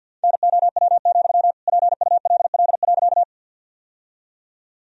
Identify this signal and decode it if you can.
Morse code, heard as a keyed tone, about 50 wpm, 700 Hz. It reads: NOW, PULL.
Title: Sentences from Top 1000 Words - Review Part 2 - 50wpm